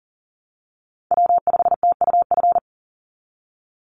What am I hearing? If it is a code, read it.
Morse code, heard as a keyed tone, about 40 wpm, 700 Hz. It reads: W5TUF
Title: Call Signs Repeated 3x - 40wpm